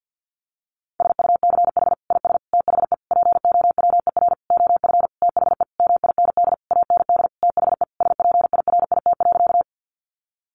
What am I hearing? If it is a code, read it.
Morse code, heard as a keyed tone, about 48 wpm, 700 Hz. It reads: SUCH IS THE POWER OF THE MIND AND THE SPIRIT.
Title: Sentences from Top 1400 Words 50wpm